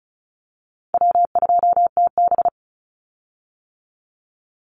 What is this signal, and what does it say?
Morse code, heard as a keyed tone, about 36 wpm, 700 Hz. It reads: W2TB
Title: Call Signs - 35wpm